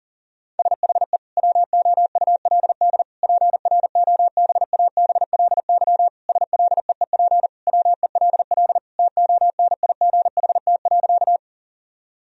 Morse code, heard as a keyed tone, about 40 wpm, 700 Hz. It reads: SHE WOULD PROBABLY SLEEP WELL TONIGHT.
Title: Sentences from Top 1000 Words - Review Part 4 - 40wpm